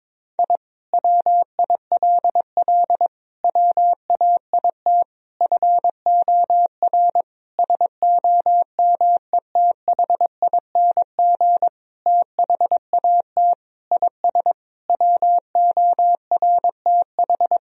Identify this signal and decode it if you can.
Morse code, heard as a keyed tone, about 22 wpm, 700 Hz. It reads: I WILL WAIT FOR SOMETHING THAT IS WORTH
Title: Sentences from Top 1000 Words - Review Part 1 - 22wpm